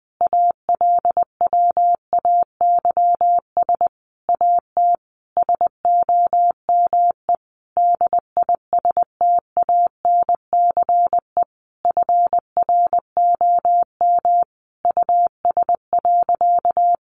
Morse code, heard as a keyed tone, about 19 wpm, 700 Hz. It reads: ALWAYS AT SOME DISTANCE FROM US.
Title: Sentences from Top 1200 Words 20wpm